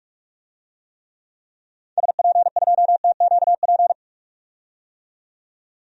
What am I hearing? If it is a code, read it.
Morse code, heard as a keyed tone, about 41 wpm, 700 Hz. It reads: SP2TQP
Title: DXCC-100 Call Signs - 6 Characters - 45wpm